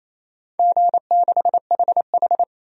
Morse code, heard as a keyed tone, about 28 wpm, 700 Hz. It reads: G6HH
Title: DXCC-100 Call Signs - 4 Characters - 28wpm